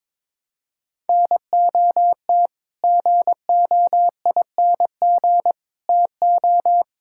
Morse code, heard as a keyed tone, about 21 wpm, 700 Hz. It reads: NOT GOING TO
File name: practice / 22wpm / Sets of 6 Words - 22wpm